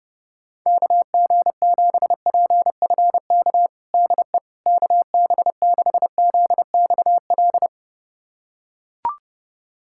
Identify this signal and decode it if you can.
Morse code, heard as a keyed tone, about 31 wpm, 700 Hz. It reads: KG7PFK DE KB6ZXL
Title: USA Callsign DE Callsign - 6 Characters - 30wpm